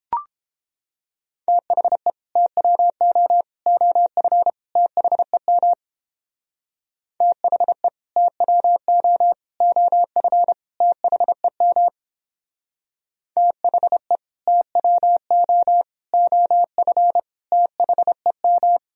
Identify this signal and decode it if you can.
Morse code, heard as a keyed tone, about 29 wpm, 700 Hz. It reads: THE TWO OF THEM THE TWO OF THEM THE TWO OF THEM